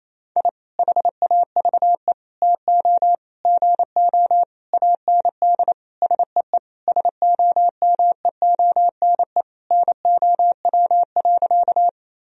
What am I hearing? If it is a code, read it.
Morse code, heard as a keyed tone, about 28 wpm, 700 Hz. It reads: I HAVE TO GO AND SEE SOMEONE NOW.
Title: Sentences from Top 1000 Words - Review Part 4 - 28wpm